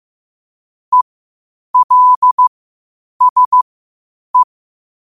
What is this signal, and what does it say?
Morse code, heard as a keyed tone, about 15 wpm, spaced as for 8 wpm, 1000 Hz. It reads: ELSE
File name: qotdfiles / 8wpm